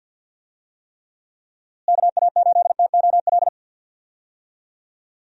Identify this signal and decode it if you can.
Morse code, heard as a keyed tone, about 46 wpm, 700 Hz. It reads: KA8TOL
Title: USA Call Signs - 6 Characters - Repeated 3x - 50wpm